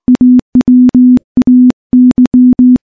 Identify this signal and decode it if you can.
Morse code, heard as a keyed tone, about 21 wpm, 250 Hz. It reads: AWAY
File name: morse / morse-training-20WPM